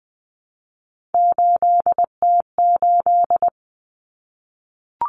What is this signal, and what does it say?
Morse code, heard as a keyed tone, about 20 wpm, 700 Hz. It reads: 8T8
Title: Common Cut Numbers - 3-Digit 20wpm